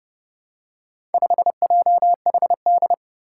Morse code, heard as a keyed tone, about 32 wpm, 700 Hz. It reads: 5JHD
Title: Four Challenging Letter-Number Combinations 30wpm